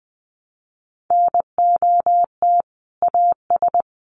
Morse code, heard as a keyed tone, about 20 wpm, 700 Hz. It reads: NOT AS
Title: Sentences from Top 1200 Words 20wpm